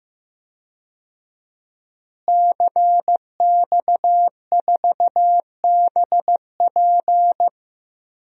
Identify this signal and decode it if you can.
Morse code, heard as a keyed tone, about 15 wpm, 700 Hz. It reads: CX4BP
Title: DXCC-100 Call Signs - 5 Characters - Repeated 3x - 15wpm